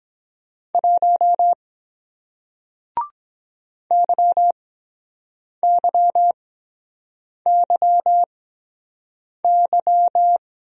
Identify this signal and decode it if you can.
Morse code, heard as a keyed tone, about 23 wpm, spaced as for 7 wpm, 700 Hz. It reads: 1 YYYY